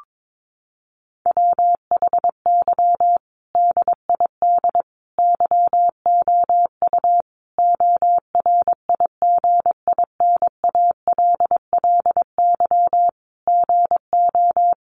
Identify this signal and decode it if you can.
Morse code, heard as a keyed tone, about 22 wpm, 700 Hz. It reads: WHY DID YOU ORIGINALLY GO
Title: Sentences from Top 1500 Words 22wpm